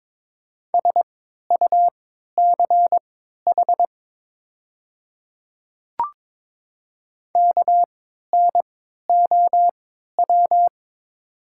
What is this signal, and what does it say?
Morse code, heard as a keyed tone, about 22 wpm, spaced as for 12 wpm, 690 Hz. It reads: SUCH KNOW
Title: 3x Character Spacing - Top 200 Words - 22wpm